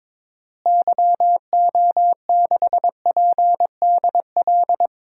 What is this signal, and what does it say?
Morse code, heard as a keyed tone, about 22 wpm, 700 Hz. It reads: YO6PDL
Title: DXCC-100 Call Signs - 6 Characters - Repeated 2x - 22wpm